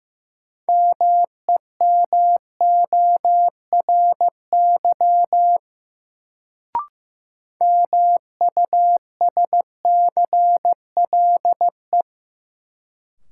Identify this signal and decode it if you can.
Morse code, heard as a keyed tone, about 15 wpm, 700 Hz. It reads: MEMORY MUSCLE
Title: Sentences from Top 1300 Words - New Word Drill - 15wpm